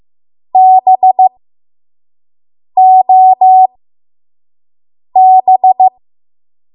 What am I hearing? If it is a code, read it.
Morse code, heard as a keyed tone, about 15 wpm, spaced as for 5 wpm, 750 Hz. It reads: BOB
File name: Morse / 05WPM